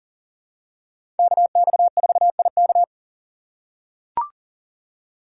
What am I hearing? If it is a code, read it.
Morse code, heard as a keyed tone, about 40 wpm, 700 Hz. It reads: KX4IK
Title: USA Call Signs - 5 Characters - 4x Character Spacing - 40wpm